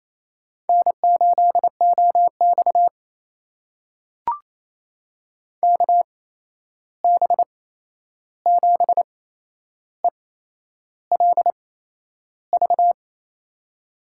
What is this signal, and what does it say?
Morse code, heard as a keyed tone, about 28 wpm, spaced as for 11 wpm, 700 Hz. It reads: N8OX KB7ELV
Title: USA Call Signs - 8x Character Spacing - 28wpm